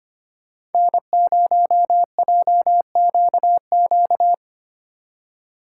Morse code, heard as a keyed tone, about 31 wpm, 700 Hz. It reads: N0JQQ
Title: Speed Racing - USA 1x3 Call Signs 25wpm